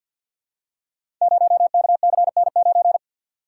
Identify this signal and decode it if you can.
Morse code, heard as a keyed tone, about 48 wpm, 700 Hz. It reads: 0KCN9
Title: Code Groups - 5 Letter:Numbers - 50wpm